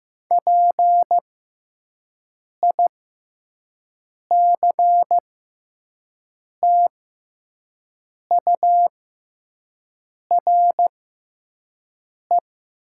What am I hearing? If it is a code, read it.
Morse code, heard as a keyed tone, about 15 wpm, spaced as for 5 wpm, 700 Hz. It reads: PICTURE
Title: 6x Character Spacing - Top 200 Words - 15wpm